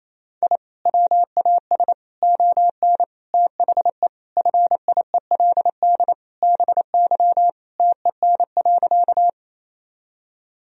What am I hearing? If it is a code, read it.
Morse code, heard as a keyed tone, about 28 wpm, 700 Hz. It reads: I WAS ON THE FIELD BY TEN.
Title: Sentences from Top 1000 Words - Review Part 2 - 28wpm